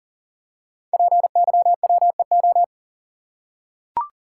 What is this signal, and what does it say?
Morse code, heard as a keyed tone, about 40 wpm, 700 Hz. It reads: PYWEO
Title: Code Groups - 5 Letters - 40wpm